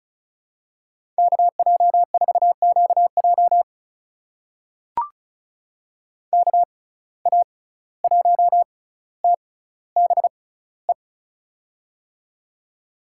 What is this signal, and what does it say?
Morse code, heard as a keyed tone, about 36 wpm, spaced as for 17 wpm, 700 Hz. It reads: KJ4QJ KA1TBE
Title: USA Call Signs - 6x Character Spacing - 35wpm